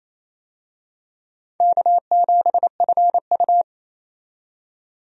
Morse code, heard as a keyed tone, about 28 wpm, 700 Hz. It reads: K7FU
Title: USA Call Signs - 4 Characters - 28wpm